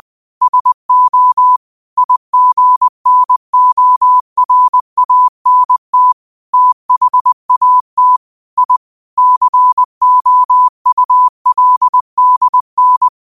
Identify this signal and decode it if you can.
Morse code, heard as a keyed tone, about 20 wpm, 1000 Hz. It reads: SO IGNORANT THAT I COULDN